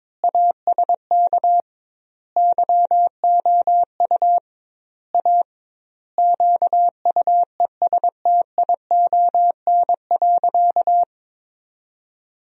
Morse code, heard as a keyed tone, about 21 wpm, 700 Hz. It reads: ASK YOU A QUESTION.
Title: 2x Word-Spacing - Sentences from Top 400 Words - 22wpm